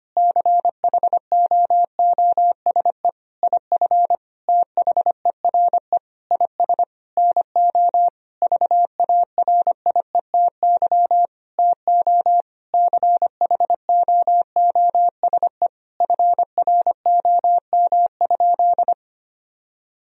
Morse code, heard as a keyed tone, about 25 wpm, 700 Hz. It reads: CHOOSE IF THERE IS NO VARIETY TO CHOOSE FROM?
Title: Sentences from Top 1000 Words - Review Part 4 - 25wpm